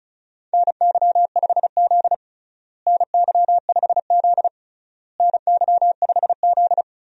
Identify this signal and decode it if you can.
Morse code, heard as a keyed tone, about 36 wpm, 700 Hz. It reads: NY5Z NY5Z NY5Z
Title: USA Call Signs - 4 Characters - Repeated 3x - 35wpm